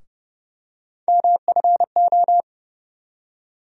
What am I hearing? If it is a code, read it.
Morse code, heard as a keyed tone, about 30 wpm, 700 Hz. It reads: MFO
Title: USA Call Sign Postfixes - 3 Letters 30wpm